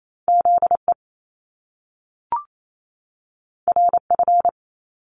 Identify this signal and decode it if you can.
Morse code, heard as a keyed tone, about 28 wpm, 700 Hz. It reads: ZE RF
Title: Top Two Letter Combinations in Words 28wpm